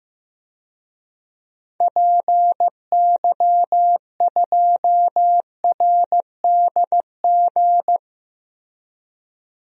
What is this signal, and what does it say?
Morse code, heard as a keyed tone, about 15 wpm, 700 Hz. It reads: PY2RDG
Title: DXCC-100 Call Signs - 6 Characters - Repeated 3x - 15wpm